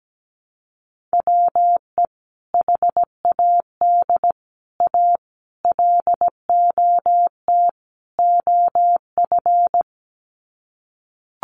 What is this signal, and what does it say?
Morse code, heard as a keyed tone, about 17 wpm, 700 Hz. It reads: WE HAD A LOT OF
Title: Sets of 5 Words Encore 17wpm